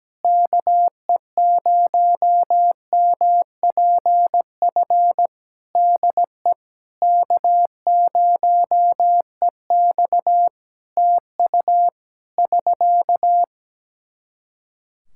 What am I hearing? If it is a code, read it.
Morse code, heard as a keyed tone, about 17 wpm, 700 Hz. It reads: KE0MPF DE K0EX TU <SK>
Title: Practice QSO - Exchange 4 - Ending - 17wpm